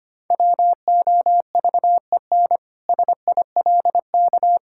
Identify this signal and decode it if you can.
Morse code, heard as a keyed tone, about 25 wpm, 700 Hz. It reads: WOVEN SILK
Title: Pangram Challenge v2 - 25wpm